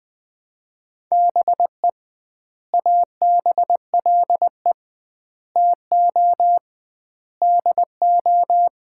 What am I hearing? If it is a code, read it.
Morse code, heard as a keyed tone, about 20 wpm, 700 Hz. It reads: BE ABLE TO DO
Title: 2x Word-Spacing - Sets of 4 Words - 20wpm